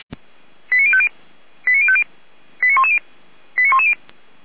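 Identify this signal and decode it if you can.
Five-tone selective call (ZVEI), heard as quick runs of five tones, tones of about 70 ms. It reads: 89050 89050 89110 89110
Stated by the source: ZVEI five-tone selective call received by SDR.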